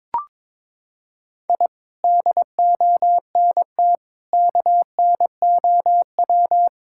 Morse code, heard as a keyed tone, about 22 wpm, 700 Hz. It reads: I DONT KNOW
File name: practice / 22wpm / Sets of 7 Words - 22wpm